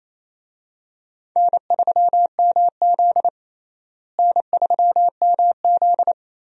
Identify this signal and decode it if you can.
Morse code, heard as a keyed tone, about 28 wpm, 700 Hz. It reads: N3MZ N3MZ
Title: USA Call Signs - 4 Characters - Repeated 2x - 28wpm